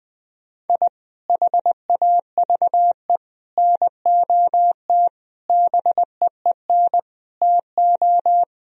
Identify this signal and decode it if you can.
Morse code, heard as a keyed tone, about 20 wpm, 700 Hz. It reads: I HAVE NOT BEEN TO
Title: Sentences from Top 1000 Words - Review Part 1 - 20wpm